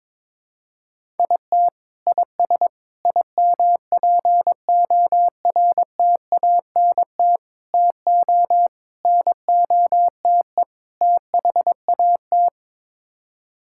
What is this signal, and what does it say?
Morse code, heard as a keyed tone, about 22 wpm, 700 Hz. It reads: IT IS IMPORTANT TO NOTE THAT
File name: practice / 22wpm / Sets of 6 Words - 22wpm